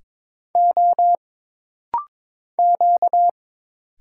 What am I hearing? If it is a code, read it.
Morse code, heard as a keyed tone, about 22 wpm, 700 Hz. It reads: O Q
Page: This practice set is composed of Letters, Numbers, common punctuations, and common prosigns.